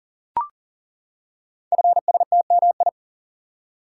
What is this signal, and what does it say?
Morse code, heard as a keyed tone, about 41 wpm, 690 Hz. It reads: FSTMI
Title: Code Groups - 5 Letters - 40wpm